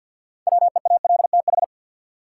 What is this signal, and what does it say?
Morse code, heard as a keyed tone, about 52 wpm, 700 Hz. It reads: WEALTH
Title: Six Letter Words - 50wpm